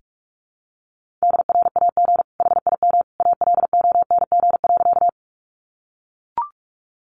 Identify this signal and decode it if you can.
Morse code, heard as a keyed tone, about 44 wpm, 700 Hz. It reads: DRAG HIM ALONG.
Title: Sentences from Top 1500 Words 45wpm